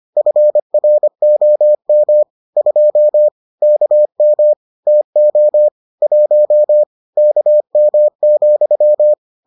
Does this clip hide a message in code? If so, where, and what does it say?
Yes: Morse code from 0.2 to 9.1 s, heard as a keyed tone, about 25 wpm, 600 Hz. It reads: FROM 2 KM TO 1 KM,